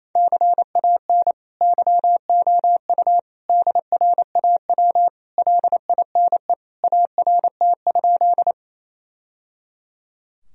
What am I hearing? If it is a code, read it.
Morse code, heard as a keyed tone, about 28 wpm, 700 Hz. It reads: CAN YOU DRAW LINE ART?
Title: Sentences from Top 1000 Words - Review Part 3 - 28wpm